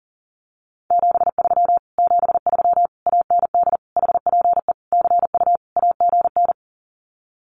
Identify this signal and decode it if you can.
Morse code, heard as a keyed tone, about 40 wpm, 700 Hz. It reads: 73 73 AND HPE CU AGN
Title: Common QSO Phrases Encore 40wpm